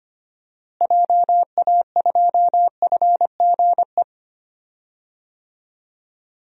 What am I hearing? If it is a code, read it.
Morse code, heard as a keyed tone, about 25 wpm, 700 Hz. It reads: JA2FGE
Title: DXCC-100 Call Signs - 6 Characters - Repeated 2x - 25wpm